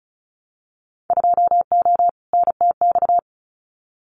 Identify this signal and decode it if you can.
Morse code, heard as a keyed tone, about 34 wpm, 700 Hz. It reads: 2O NTX
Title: Winter Field Day Exchanges 35wpm